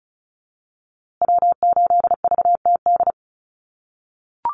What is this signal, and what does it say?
Morse code, heard as a keyed tone, about 34 wpm, 700 Hz. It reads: W8VTD
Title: USA Call Signs Repeated 3x 35wpm